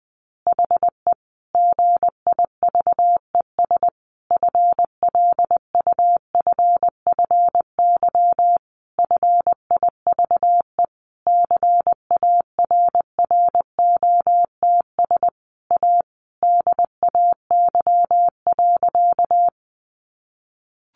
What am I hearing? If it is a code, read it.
Morse code, heard as a keyed tone, about 20 wpm, 700 Hz. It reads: HE GIVES FLUFFY FIVE CARROTS A DAY.
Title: Easy Phrases 20wpm